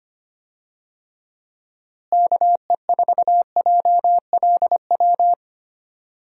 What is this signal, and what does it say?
Morse code, heard as a keyed tone, about 25 wpm, 700 Hz. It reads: KE4JLW
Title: USA Call Signs - 6 Characters - Repeated 2x - 25wpm